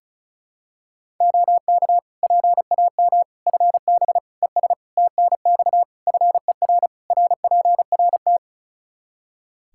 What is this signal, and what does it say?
Morse code, heard as a keyed tone, about 34 wpm, 700 Hz. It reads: OK PAM FB ES TNX FER RPRT
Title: K1USN SST QSO Practice 35wpm